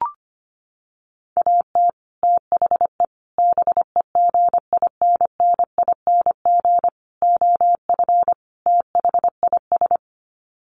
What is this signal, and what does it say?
Morse code, heard as a keyed tone, about 25 wpm, 700 Hz. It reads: AT THE BEGINNING OF THIS
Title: Sets of 5 Words Encore 25wpm